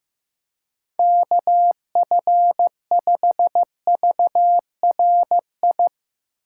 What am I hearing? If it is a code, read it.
Morse code, heard as a keyed tone, about 17 wpm, 700 Hz. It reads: KF5VRI
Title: Speed Racing - USA 2x3 Call Signs 15wpm